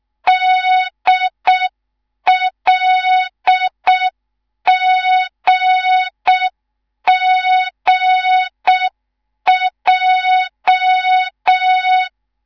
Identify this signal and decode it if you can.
Morse code, heard as a keyed tone, about 6 wpm, 3700 Hz. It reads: DLGGJ
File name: Morze-HA2MN / hang / 19 GYAKORLAT